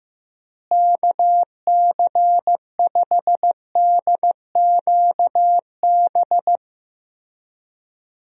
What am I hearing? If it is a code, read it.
Morse code, heard as a keyed tone, about 15 wpm, 700 Hz. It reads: KC5DQB
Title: USA Call Signs - 6 Characters - Repeated 3x - 15wpm